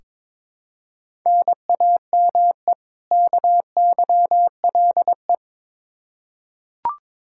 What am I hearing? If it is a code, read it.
Morse code, heard as a keyed tone, about 21 wpm, 700 Hz. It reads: NAME KYLE
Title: Practice QSO - Exchange 1 - Beginning - 22wpm